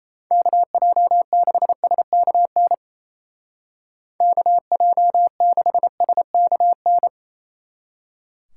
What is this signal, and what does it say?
Morse code, heard as a keyed tone, about 34 wpm, 700 Hz. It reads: KJ6SKN KJ6SKN
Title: Speed Racing - USA 2x3 Call Signs 28wpm